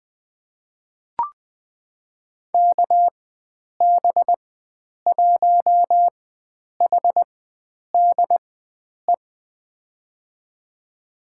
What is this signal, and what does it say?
Morse code, heard as a keyed tone, about 21 wpm, spaced as for 9 wpm, 700 Hz. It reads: KB1HDE
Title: USA Call Signs - 6 Characters - 4x Character Spacing - 20wpm